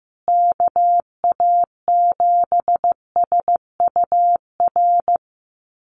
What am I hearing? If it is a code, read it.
Morse code, heard as a keyed tone, about 15 wpm, 700 Hz. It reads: KA7SUR
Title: USA Call Signs Repeated 2x 15wpm